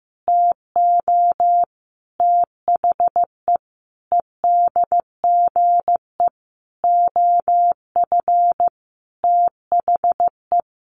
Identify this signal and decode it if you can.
Morse code, heard as a keyed tone, about 15 wpm, 700 Hz. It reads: TO THE EDGE OF THE
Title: Sentences from Top 1300 Words 15wpm